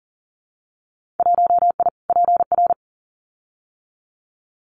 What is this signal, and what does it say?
Morse code, heard as a keyed tone, about 40 wpm, 700 Hz. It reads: 1I PR
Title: Winter Field Day Exchanges Encore 40wpm